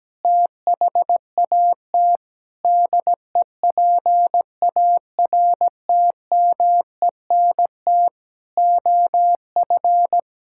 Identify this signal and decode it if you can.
Morse code, heard as a keyed tone, about 17 wpm, 700 Hz. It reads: THAT DEPARTMENT OF
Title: Sentences from Top 1000 Words - Review Part 5 - 17wpm